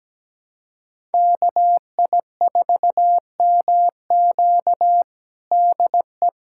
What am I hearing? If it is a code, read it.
Morse code, heard as a keyed tone, about 17 wpm, 700 Hz. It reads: KI4MQ DE
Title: USA Callsign DE Callsign - 5 Characters - 17wpm